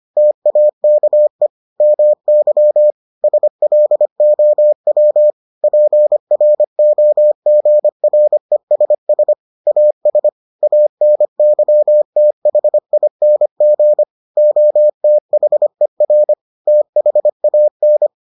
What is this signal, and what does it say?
Morse code, heard as a keyed tone, about 25 wpm, 600 Hz. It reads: TAKE MY SLOW PROGRESS AS ANYTHING OTHER THAN